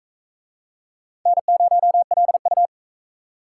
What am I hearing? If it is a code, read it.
Morse code, heard as a keyed tone, about 43 wpm, 700 Hz. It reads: N0LU